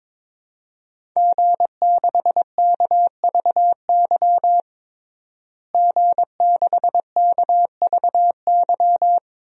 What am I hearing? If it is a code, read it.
Morse code, heard as a keyed tone, about 22 wpm, 700 Hz. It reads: G6KVY G6KVY
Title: DXCC-100 Call Signs - 5 Characters - Repeated 2x - 22wpm